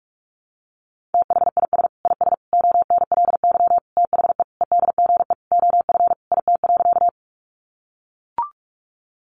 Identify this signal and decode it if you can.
Morse code, heard as a keyed tone, about 44 wpm, 700 Hz. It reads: THIS IS ONLY THE EDGE OF IT.
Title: Sentences from Top 1300 Words 45wpm